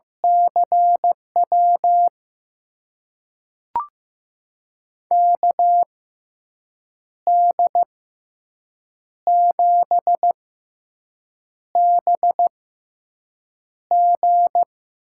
Morse code, heard as a keyed tone, about 15 wpm, spaced as for 6 wpm, 700 Hz. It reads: CW KD7BG
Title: USA Call Signs - 6 Characters - 6x Character Spacing - 15wpm